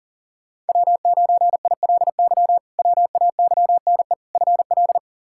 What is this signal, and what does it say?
Morse code, heard as a keyed tone, about 40 wpm, 700 Hz. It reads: W9ILY WAYNE FL
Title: K1USN SST QSO Practice - 40wpm